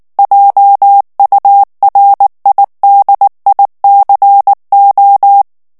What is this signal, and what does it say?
Morse code, heard as a keyed tone, about 19 wpm, 800 Hz. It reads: JURIDICO
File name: morse mp3 / ESPAGNOL TEXTE EN CLAIR 2 VITESSE 18 MOTS MINUTES